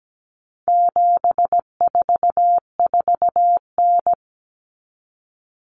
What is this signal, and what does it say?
Morse code, heard as a keyed tone, about 17 wpm, 700 Hz. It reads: 744N
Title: Common Cut Numbers - 4-Digit 17wpm